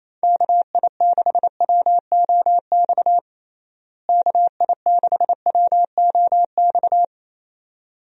Morse code, heard as a keyed tone, about 28 wpm, 700 Hz. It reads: KI6WOX KI6WOX
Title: USA Call Signs - 6 Characters - Repeated 2x - 28wpm